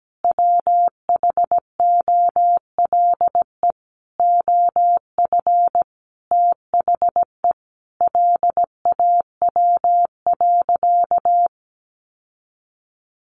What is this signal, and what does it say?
Morse code, heard as a keyed tone, about 17 wpm, 700 Hz. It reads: WHOLE OF THE LAW.
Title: Sentences from Top 1400 Words 17wpm